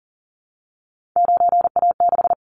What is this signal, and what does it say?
Morse code, heard as a keyed tone, about 40 wpm, 700 Hz. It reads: 9A6
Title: Callsign Prefixes for Top 25 DXCC - Repeated 3x - 40wpm